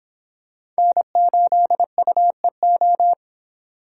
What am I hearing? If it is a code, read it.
Morse code, heard as a keyed tone, about 25 wpm, 700 Hz. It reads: N8UEO